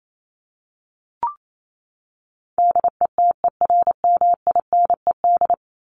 Morse code, heard as a keyed tone, about 28 wpm, 700 Hz. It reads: DETERMINED
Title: ED Suffix - Top 100 Words with Base Words 28wpm